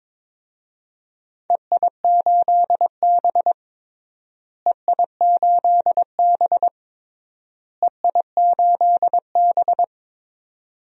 Morse code, heard as a keyed tone, about 22 wpm, 700 Hz. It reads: EI8B EI8B EI8B
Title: DXCC-100 Call Signs - 4 Characters - Repeated 3x - 22wpm